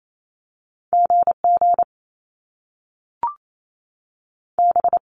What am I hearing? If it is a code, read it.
Morse code, heard as a keyed tone, about 28 wpm, 700 Hz. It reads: GG B
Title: Top Two Letter Combinations in Words 28wpm